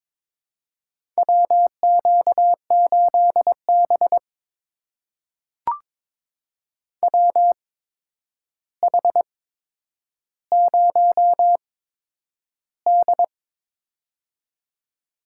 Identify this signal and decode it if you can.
Morse code, heard as a keyed tone, about 22 wpm, spaced as for 9 wpm, 700 Hz. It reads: WQ8B WH0D
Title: USA Call Signs - 4 Characters - 8x Character Spacing - 22wpm